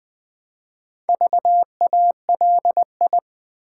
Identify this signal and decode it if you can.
Morse code, heard as a keyed tone, about 20 wpm, 700 Hz. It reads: VALI
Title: Build A Word - 1 Letter At A Time - 3 to 5-Letter Words - 20wpm